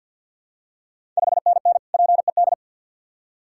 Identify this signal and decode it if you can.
Morse code, heard as a keyed tone, about 50 wpm, 700 Hz. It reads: 5NN PED
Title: Ontario QSO Party Exchange 50wpm